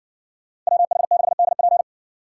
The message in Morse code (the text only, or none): WH6DP